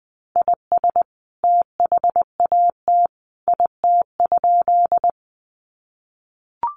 Morse code, heard as a keyed tone, about 20 wpm, 700 Hz. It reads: IS THAT IT?
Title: Sentences from Top 100 Words 20wpm